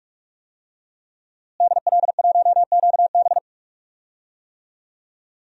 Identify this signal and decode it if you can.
Morse code, heard as a keyed tone, about 44 wpm, 700 Hz. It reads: DL1QB
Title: DXCC-100 Call Signs - 5 Characters - Repeated 2x - 45wpm